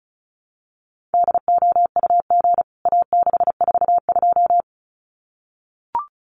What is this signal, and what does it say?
Morse code, heard as a keyed tone, about 34 wpm, 700 Hz. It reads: DOUG A642
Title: CWT Exchange 35wpm